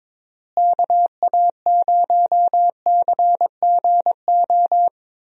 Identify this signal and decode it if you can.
Morse code, heard as a keyed tone, about 22 wpm, 700 Hz. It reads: KA0CGO
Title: USA Call Signs - 6 Characters - Repeated 3x - 22wpm